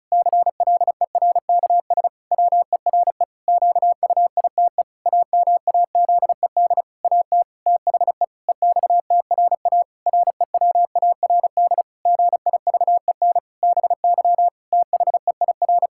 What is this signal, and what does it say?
Morse code, heard as a keyed tone, about 36 wpm, 700 Hz. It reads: CLERKS WERE QUITE AMAZED AT THE EXTRA REWARD GIVEN BY THEIR
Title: Pangram Challenge v2 - 35wpm